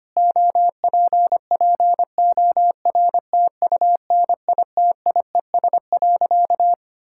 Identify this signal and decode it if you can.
Morse code, heard as a keyed tone, about 25 wpm, 700 Hz. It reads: OPPORTUNITIES.
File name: practice / 25wpm / Difficult Sentences with Hints - 25wpm